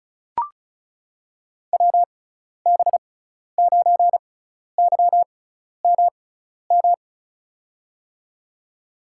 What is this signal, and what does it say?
Morse code, heard as a keyed tone, about 36 wpm, spaced as for 12 wpm, 700 Hz. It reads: WB9YMM
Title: USA Call Signs - 6x Character Spacing - 35wpm